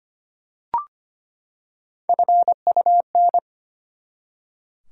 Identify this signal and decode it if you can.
Morse code, heard as a keyed tone, about 25 wpm, 700 Hz. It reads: FUN
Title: Three Letter Words - 25wpm